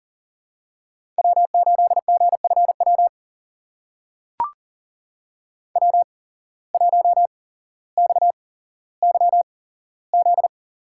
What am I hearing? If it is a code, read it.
Morse code, heard as a keyed tone, about 40 wpm, spaced as for 17 wpm, 700 Hz. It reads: W8GFW W1XYZ
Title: USA Call Signs - 5 Characters - 8x Character Spacing - 40wpm